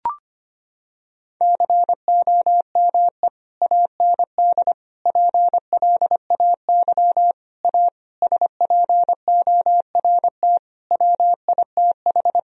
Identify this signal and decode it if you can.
Morse code, heard as a keyed tone, about 25 wpm, 700 Hz. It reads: COME AND PLAY A SPORT WITH
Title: Sentences from Top 1000 Words - Review Part 4 - 25wpm